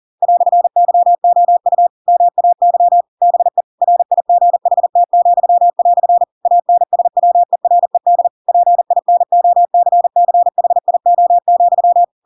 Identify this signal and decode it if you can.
Morse code, heard as a keyed tone, about 40 wpm, 700 Hz. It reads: "YOU MAY BE RIGHT," ANSWERED PINOCCHIO,
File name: Pinocchio / Pinocchio ch7 40 40 ik1hge